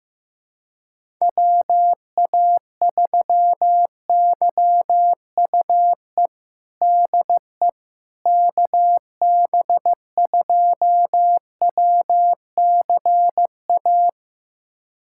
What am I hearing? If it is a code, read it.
Morse code, heard as a keyed tone, about 15 wpm, 700 Hz. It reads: WA3YUE DE KB2WCA
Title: USA Callsign DE Callsign 15wpm